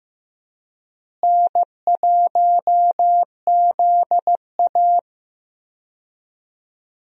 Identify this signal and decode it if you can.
Morse code, heard as a keyed tone, about 16 wpm, 700 Hz. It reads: N1ZA
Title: USA Call Signs - 4 Characters - Repeated 2x - 15wpm